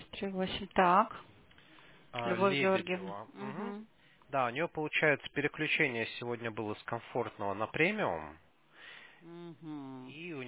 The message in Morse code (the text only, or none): none